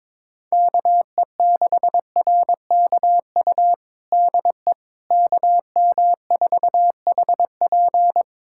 Morse code, heard as a keyed tone, about 22 wpm, 700 Hz. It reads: KE6RKU DE KM4HP
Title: USA Callsign DE Callsign 22wpm